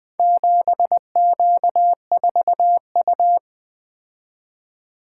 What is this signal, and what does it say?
Morse code, heard as a keyed tone, about 25 wpm, 700 Hz. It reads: 7Q4U
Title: Speed-Racing - Four Letter-Number Combinations - 20wpm